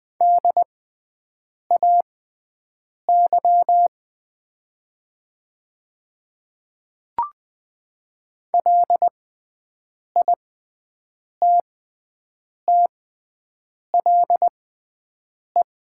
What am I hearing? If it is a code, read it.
Morse code, heard as a keyed tone, about 20 wpm, spaced as for 7 wpm, 700 Hz. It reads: DAY LITTLE
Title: 6x Character Spacing - Top 300 Words - 20wpm